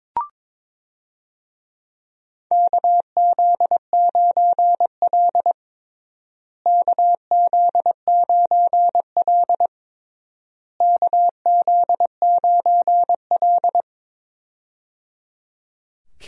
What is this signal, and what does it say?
Morse code, heard as a keyed tone, about 22 wpm, 700 Hz. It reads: KZ9L KZ9L KZ9L
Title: USA Call Signs - 4 Characters - Repeated 3x - 22wpm